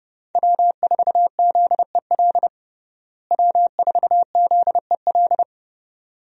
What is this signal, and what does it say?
Morse code, heard as a keyed tone, about 32 wpm, 700 Hz. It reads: W4ZEL W4ZEL
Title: USA Call Signs - 5 Characters - Repeated 2x - 30wpm